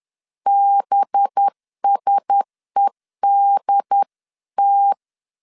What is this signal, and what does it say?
Morse code, heard as a keyed tone, about 10 wpm, 800 Hz. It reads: BSEDT
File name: LEZIONI / 6 PKX0 / 6 P 10wpm 2432